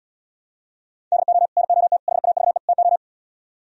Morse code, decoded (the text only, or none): DO NOT HAVE TO